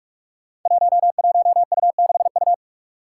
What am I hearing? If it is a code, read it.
Morse code, heard as a keyed tone, about 42 wpm, 700 Hz. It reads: 11U6U
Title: Five Challenging Letter-Number Combinations 45wpm